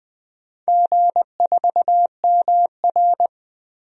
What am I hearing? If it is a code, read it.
Morse code, heard as a keyed tone, about 20 wpm, 700 Hz. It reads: G4MR
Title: DXCC-100 Call Signs - 4 Characters - Repeated 3x - 20wpm